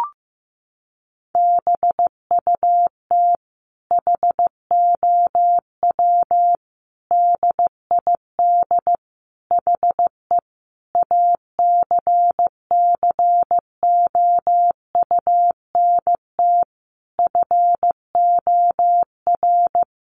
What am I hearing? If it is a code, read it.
Morse code, heard as a keyed tone, about 14 wpm, 700 Hz. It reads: BUT HOW DID HE ACCOUNT FOR
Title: Sentences from Top 900 Words 15wpm